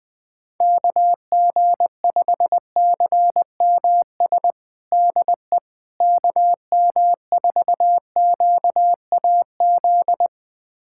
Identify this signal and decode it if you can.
Morse code, heard as a keyed tone, about 21 wpm, 700 Hz. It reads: KG5CMS DE KM4QAZ
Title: USA Callsign DE Callsign - 6 Characters - 20wpm